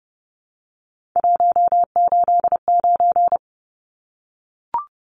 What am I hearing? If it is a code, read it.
Morse code, heard as a keyed tone, about 30 wpm, 700 Hz. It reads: 189
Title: Numbers - 3-Digit 30wpm